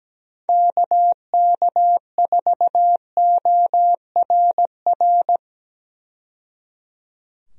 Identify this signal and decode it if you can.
Morse code, heard as a keyed tone, about 17 wpm, 700 Hz. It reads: KK4ORR
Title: USA Call Signs - 6 Characters - 17wpm